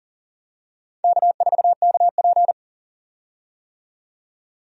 Morse code, heard as a keyed tone, about 40 wpm, 700 Hz. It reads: K4KP
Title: USA Call Signs - 4 Characters - 40wpm